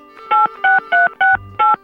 DTMF, heard as telephone touch tones, tones of about 170 ms. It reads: 86368